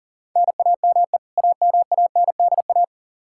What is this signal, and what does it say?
Morse code, heard as a keyed tone, about 40 wpm, 700 Hz. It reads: NAME AMANDA
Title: Practice QSO - Exchange 1 - Beginning - 40wpm